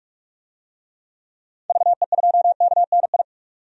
Morse code, heard as a keyed tone, about 46 wpm, 700 Hz. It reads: VE2KNI
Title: DXCC-100 Call Signs - 6 Characters - 45wpm